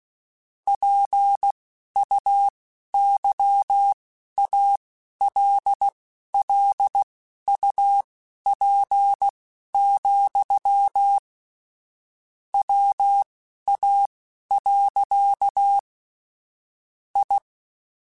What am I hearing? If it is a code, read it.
Morse code, heard as a keyed tone, about 16 wpm, spaced as for 11 wpm, 800 Hz. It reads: PUYALLUP, WA. I
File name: QSOs / QSOs-mp3s / kc7haa@10